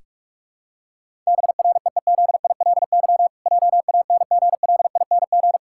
Morse code, heard as a keyed tone, about 46 wpm, 700 Hz. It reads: BREEZILY JANGLING
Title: Pangram Challenge v2 - 45wpm